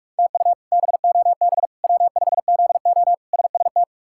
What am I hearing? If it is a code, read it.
Morse code, heard as a keyed tone, about 46 wpm, 700 Hz. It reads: TU BOB W5ZO SST
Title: K1USN SST QSO Practice - 45wpm